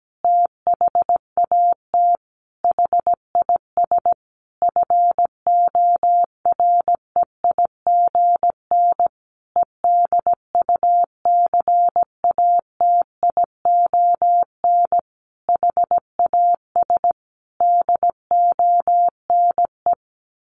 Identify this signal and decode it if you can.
Morse code, heard as a keyed tone, about 17 wpm, 700 Hz. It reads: THAT HIS FOREIGN EDUCATION HAS DONE
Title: Sentences from Top 1400 Words 17wpm